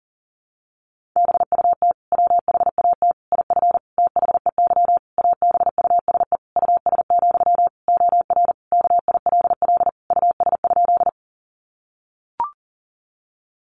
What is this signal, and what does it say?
Morse code, heard as a keyed tone, about 40 wpm, 700 Hz. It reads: BUT WHAT IF THEY ABUSE US, OR KILL US?
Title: Sentences from Top 1400 Words 40wpm